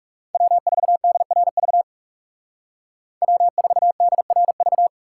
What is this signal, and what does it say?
Morse code, heard as a keyed tone, about 39 wpm, 700 Hz. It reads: W4DRV W4DRV